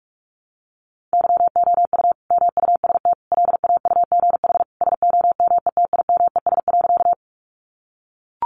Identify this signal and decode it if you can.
Morse code, heard as a keyed tone, about 44 wpm, 700 Hz. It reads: YOU MUST LAUGH SOMETIMES.
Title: Sentences from Top 600 Words 45wpm